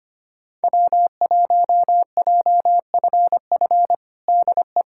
Morse code, heard as a keyed tone, about 25 wpm, 700 Hz. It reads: W1JFF DE
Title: USA Callsign DE Callsign 25wpm